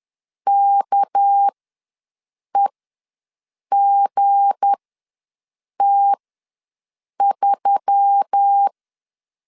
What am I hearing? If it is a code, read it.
Morse code, heard as a keyed tone, about 10 wpm, spaced as for 6 wpm, 800 Hz. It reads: KEGT3
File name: LEZIONI / 7 GF9Z / 7 GF9 ws 10wpm 4034